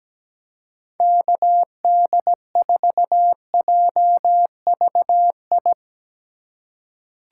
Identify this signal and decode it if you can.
Morse code, heard as a keyed tone, about 17 wpm, 700 Hz. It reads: KD4JVI
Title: USA Call Signs - 6 Characters - 4x Character Spacing - 17wpm